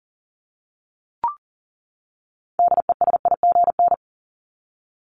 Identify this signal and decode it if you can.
Morse code, heard as a keyed tone, about 40 wpm, 700 Hz. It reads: DESIGN
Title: Top 751-1000 Words 40wpm